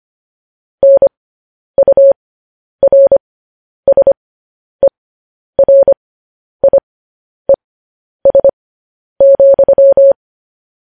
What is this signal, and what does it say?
Morse code, heard as a keyed tone, about 25 wpm, spaced as for 10 wpm, 550 Hz. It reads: NURSERIES,
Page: news headlines in morse code at 10 wpm